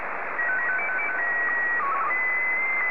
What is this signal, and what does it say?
SSTV (slow-scan television), Wraase SC2-180.
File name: sstv